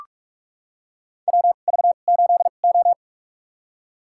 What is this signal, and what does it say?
Morse code, heard as a keyed tone, about 43 wpm, spaced as for 32 wpm, 700 Hz. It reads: WV8O
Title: USA Call Signs - 4 Characters - 2x Character Spacing - 45wpm